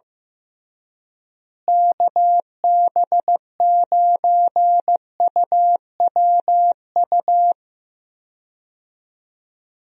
Morse code, heard as a keyed tone, about 15 wpm, 700 Hz. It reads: KB9UWU
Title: USA Call Signs - 6 Characters - Repeated 2x - 15wpm